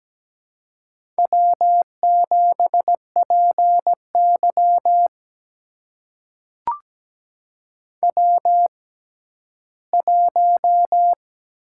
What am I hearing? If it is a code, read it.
Morse code, heard as a keyed tone, about 17 wpm, spaced as for 11 wpm, 700 Hz. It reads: W7PY W1
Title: USA Call Signs - 4 Characters - 6x Character Spacing - 17wpm